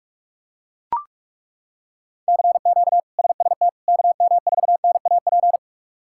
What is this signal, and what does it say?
Morse code, heard as a keyed tone, about 41 wpm, 700 Hz. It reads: CQ SST KM4NAP
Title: K1USN SST QSO Practice - 45wpm